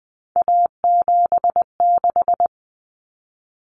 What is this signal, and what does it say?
Morse code, heard as a keyed tone, about 20 wpm, 700 Hz. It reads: A76
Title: Common Cut Numbers - 3-Digit 20wpm